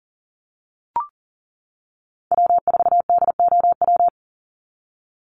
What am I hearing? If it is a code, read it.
Morse code, heard as a keyed tone, about 40 wpm, 700 Hz. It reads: W4DOW
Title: NA SOTA Chaser Callsigns 40wpm